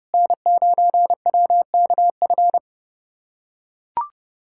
Morse code, heard as a keyed tone, about 30 wpm, 700 Hz. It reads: N9WKF
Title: USA Callsign DE Callsign - 5 Characters - 30wpm